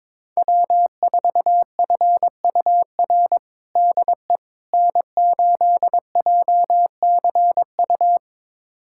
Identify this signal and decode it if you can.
Morse code, heard as a keyed tone, about 22 wpm, 700 Hz. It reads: W4FUR DE N8JCU
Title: USA Callsign DE Callsign - 5 Characters - 22wpm